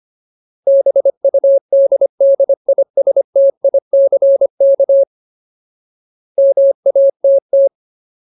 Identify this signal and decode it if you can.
Morse code, heard as a keyed tone, about 25 wpm, 550 Hz. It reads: BUDDISTICK MATT
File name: Ultimate Word List / 25WPM Ultimate